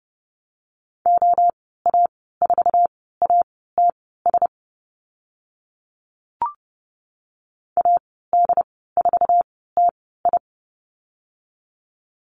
Morse code, heard as a keyed tone, about 30 wpm, spaced as for 17 wpm, 700 Hz. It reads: OA4ATS AD4TI
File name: Lesson 14g - Number 2 - Call Signs 30wpm